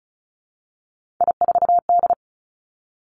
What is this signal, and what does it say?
Morse code, heard as a keyed tone, about 36 wpm, 700 Hz. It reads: I4D
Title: Three Challenging Letter-Number Combinations 35wpm